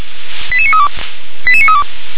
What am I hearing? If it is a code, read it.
Five-tone selective call (ZVEI), heard as quick runs of five tones, tones of about 70 ms. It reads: 88932 88932
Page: ZVEI